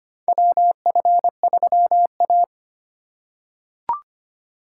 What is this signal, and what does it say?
Morse code, heard as a keyed tone, about 25 wpm, 700 Hz. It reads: WF3A
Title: USA Call Signs - 4 Characters - 4x Character Spacing - 25wpm